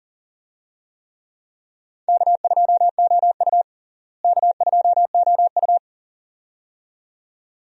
Morse code, heard as a keyed tone, about 40 wpm, 700 Hz. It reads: K2OU K2OU
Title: USA Call Signs - 4 Characters - Repeated 2x - 40wpm